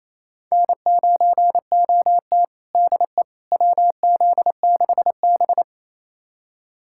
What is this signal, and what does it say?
Morse code, heard as a keyed tone, about 28 wpm, 700 Hz. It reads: N9OT DE WZ6B
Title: USA Callsign DE Callsign - 4 Characters - 28wpm